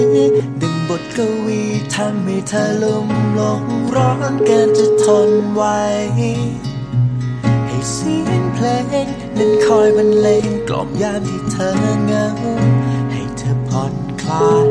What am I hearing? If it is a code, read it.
Ringback tone (Europe, 425 Hz), with noise.